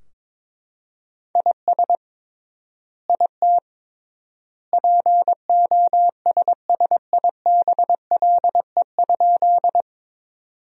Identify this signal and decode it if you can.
Morse code, heard as a keyed tone, about 22 wpm, 700 Hz. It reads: IS IT POSSIBLE?
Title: 3x Word-Spacing - Sentences from Top 500 Words - 22wpm